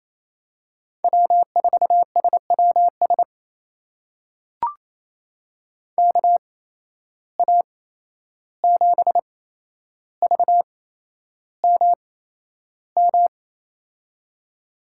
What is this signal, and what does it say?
Morse code, heard as a keyed tone, about 28 wpm, spaced as for 11 wpm, 700 Hz. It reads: W4SWS KA7VMM
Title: USA Call Signs - 8x Character Spacing - 28wpm